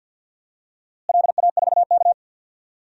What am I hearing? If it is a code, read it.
Morse code, heard as a keyed tone, about 52 wpm, 700 Hz. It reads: LA4K
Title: DXCC-100 Call Signs - 4 Characters - Repeated 2x - 50wpm